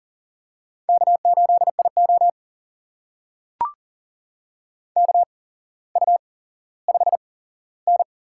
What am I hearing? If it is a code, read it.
Morse code, heard as a keyed tone, about 40 wpm, spaced as for 17 wpm, 700 Hz. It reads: K8IO KU5N
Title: USA Call Signs - 4 Characters - 8x Character Spacing - 40wpm